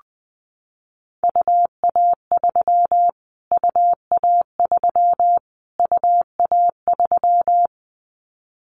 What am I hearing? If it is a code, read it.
Morse code, heard as a keyed tone, about 20 wpm, 700 Hz. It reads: UA3 UA3 UA3
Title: Callsign Prefixes for Top 25 DXCC - Repeated 3x - 20wpm